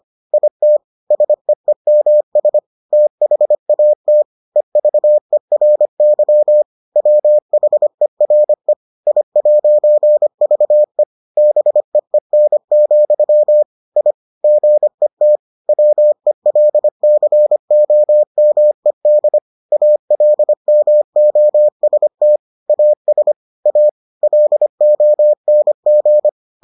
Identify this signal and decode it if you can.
Morse code, heard as a keyed tone, about 25 wpm, 600 Hz. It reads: IT SEEMS THAT EVERY WHERE I'VE BEEN, I GET WELCOMED ALMOST AS A LONG